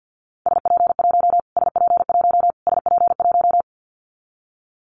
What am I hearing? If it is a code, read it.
Morse code, heard as a keyed tone, about 49 wpm, 700 Hz. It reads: SP1 SP1 SP1
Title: Callsign Prefixes for Top 25 DXCC - Repeated 3x - 50wpm